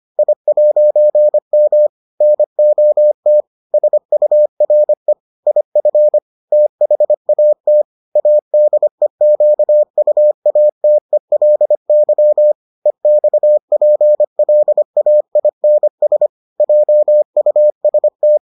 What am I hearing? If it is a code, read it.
Morse code, heard as a keyed tone, about 25 wpm, 600 Hz. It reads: I'M NOT SURE IF THAT ADEQUATELY EXPLAINS JUST